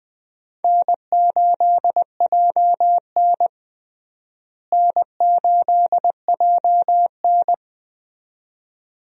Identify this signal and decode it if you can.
Morse code, heard as a keyed tone, about 20 wpm, 700 Hz. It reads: N8JN N8JN
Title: USA Call Signs - 4 Characters - Repeated 2x - 20wpm